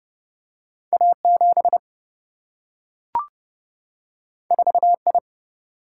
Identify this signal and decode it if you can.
Morse code, heard as a keyed tone, about 30 wpm, 700 Hz. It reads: A7 4I
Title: Single-Letter Single-Number 30wpm